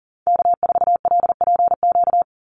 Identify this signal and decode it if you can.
Morse code, heard as a keyed tone, about 40 wpm, 700 Hz. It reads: K4LPQ
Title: NA SOTA Activator Callsigns 40wpm